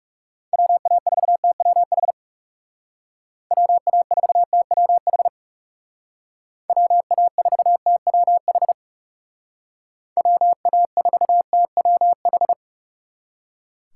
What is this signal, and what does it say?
Morse code, heard as a keyed tone, about 35 wpm, 700 Hz. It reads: WA4TWH WA4TWH WA4TWH WA4TWH
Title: Speed Racing - USA Call Signs 30wpm